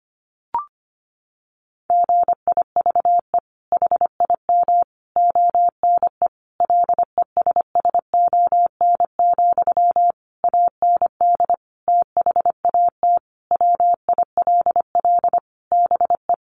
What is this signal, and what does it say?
Morse code, heard as a keyed tone, about 25 wpm, 700 Hz. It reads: GIVE HIM ONE LESSON, AND THAT WILL BE
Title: Sentences from Top 1200 Words 25wpm